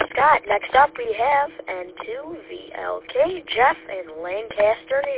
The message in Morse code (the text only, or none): none